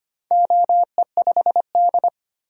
Morse code, heard as a keyed tone, about 25 wpm, 700 Hz. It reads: OE5D
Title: DXCC-100 Call Signs - 4 Characters - 25wpm